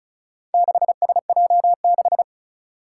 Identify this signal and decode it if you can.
Morse code, heard as a keyed tone, about 36 wpm, 700 Hz. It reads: 6SJ6
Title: Four Challenging Letter-Number Combinations 35wpm